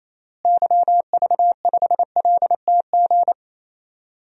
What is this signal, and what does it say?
Morse code, heard as a keyed tone, about 28 wpm, 700 Hz. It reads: YV5LTG
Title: DXCC-100 Call Signs - 6 Characters - Repeated 3x - 28wpm